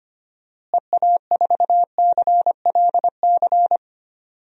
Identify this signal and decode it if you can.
Morse code, heard as a keyed tone, about 25 wpm, 700 Hz. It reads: EA4CLC
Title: DXCC-100 Call Signs - 6 Characters - Repeated 2x - 25wpm